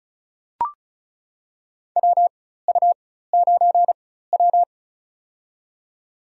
Morse code, heard as a keyed tone, about 34 wpm, spaced as for 16 wpm, 700 Hz. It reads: WU9W
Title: USA Call Signs - 4 Characters - 4x Character Spacing - 35wpm